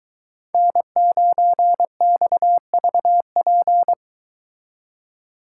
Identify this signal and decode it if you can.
Morse code, heard as a keyed tone, about 22 wpm, 700 Hz. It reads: N9XVP